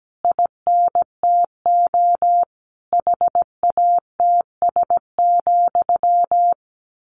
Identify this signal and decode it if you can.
Morse code, heard as a keyed tone, about 17 wpm, 700 Hz. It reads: INTO HATS,
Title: A Sound Of Thunder 17wpm